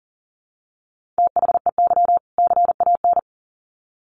Morse code, heard as a keyed tone, about 40 wpm, 700 Hz. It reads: THEY CAN
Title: Sets of 2 Words 40wpm